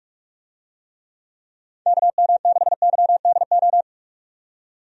Morse code, heard as a keyed tone, about 42 wpm, 700 Hz. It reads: KM6YDO
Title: USA Call Signs - 6 Characters - 45wpm